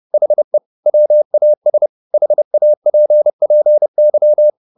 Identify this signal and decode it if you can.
Morse code, heard as a keyed tone, about 30 wpm, 600 Hz. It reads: HE WAS HAPPY
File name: the old man and the sea / the old man and the sea 1 30wpm